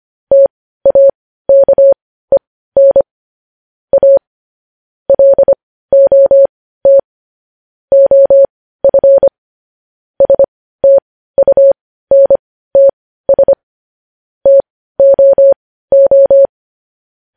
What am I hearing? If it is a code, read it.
Morse code, heard as a keyed tone, about 25 wpm, spaced as for 15 wpm, 550 Hz. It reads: TAKEN A LOT OF STUNTS TOO
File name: podcast / News headlines-1x-25x15